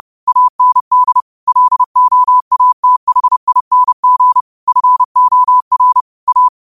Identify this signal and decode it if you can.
Morse code, heard as a keyed tone, about 30 wpm, 1000 Hz. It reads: AND LOATHING FOR A
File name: qotdfiles / 30wpm